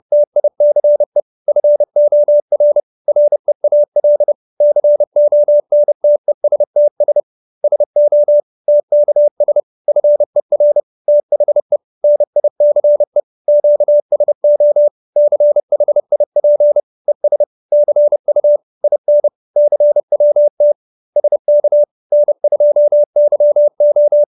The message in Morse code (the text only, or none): TICE FOR REAL CONTESTS SO TKS FER THE NICE QSO CHIP ES CU IN CWT SK N2YO